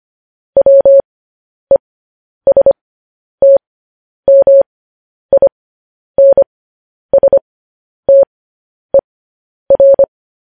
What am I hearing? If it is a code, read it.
Morse code, heard as a keyed tone, about 25 wpm, spaced as for 10 wpm, 550 Hz. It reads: WESTMINSTER